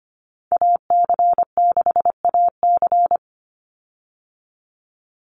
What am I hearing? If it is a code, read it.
Morse code, heard as a keyed tone, about 25 wpm, 700 Hz. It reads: AC6AC
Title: Call Signs Repeated 3x - 25wpm